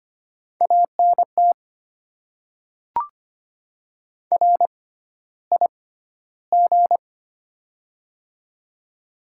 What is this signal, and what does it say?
Morse code, heard as a keyed tone, about 25 wpm, spaced as for 13 wpm, 700 Hz. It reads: ANT RIG
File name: Top 20 QSO Elements - 6x Character Spacing - 25wpm